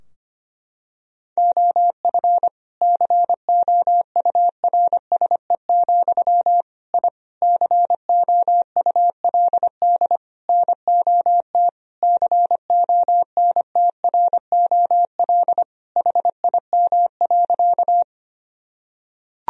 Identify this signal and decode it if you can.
Morse code, heard as a keyed tone, about 25 wpm, 700 Hz. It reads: OF COURSE, I COULD NOT CONTROL HIM.
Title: Sentences from Top 1000 Words - Review Part 3 - 25wpm